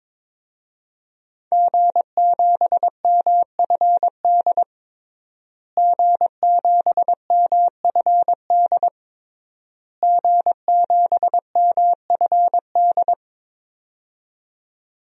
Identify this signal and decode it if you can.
Morse code, heard as a keyed tone, about 22 wpm, 700 Hz. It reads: G7MFD G7MFD G7MFD
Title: DXCC-100 Call Signs - 5 Characters - Repeated 3x - 22wpm